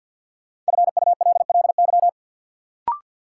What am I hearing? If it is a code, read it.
Morse code, heard as a keyed tone, about 50 wpm, 700 Hz. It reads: FULLY
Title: Sentences from Top 1300 Words - New Word Drill - 50wpm